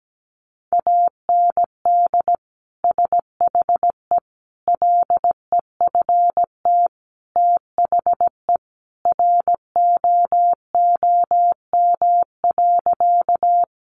Morse code, heard as a keyed tone, about 17 wpm, 700 Hz. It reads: AND SHE LEFT THE ROOM.
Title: Sentences from Top 300 Words 17wpm